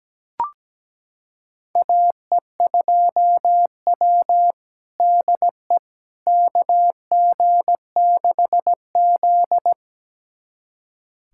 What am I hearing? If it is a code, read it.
Morse code, heard as a keyed tone, about 17 wpm, 700 Hz. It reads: AE2W DE KG6Z
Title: USA Callsign DE Callsign - 4 Characters - 17wpm